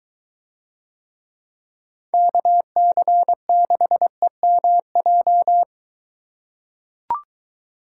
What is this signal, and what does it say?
Morse code, heard as a keyed tone, about 22 wpm, 700 Hz. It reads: KC6EMJ